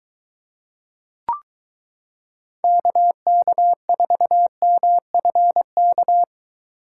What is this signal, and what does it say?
Morse code, heard as a keyed tone, about 22 wpm, 700 Hz. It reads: KK4MFK